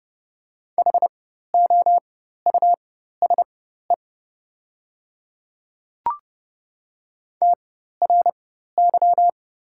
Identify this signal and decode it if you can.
Morse code, heard as a keyed tone, about 30 wpm, spaced as for 14 wpm, 690 Hz. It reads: HOUSE TRY
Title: 4x Character Spacing - Top 200 Words - 30wpm